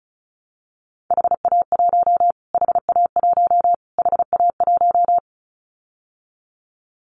Morse code, heard as a keyed tone, about 34 wpm, 700 Hz. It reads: HA1 HA1 HA1
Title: Callsign Prefixes for Top 25 DXCC - Repeated 3x - 35wpm